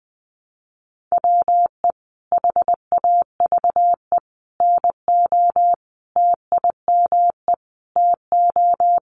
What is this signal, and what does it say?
Morse code, heard as a keyed tone, about 20 wpm, 700 Hz. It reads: WE HAVE NO TIME TO
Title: Sentences from Top 1100 Words 20wpm